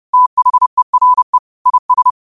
Morse code, heard as a keyed tone, about 30 wpm, 1000 Hz. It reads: THERE IS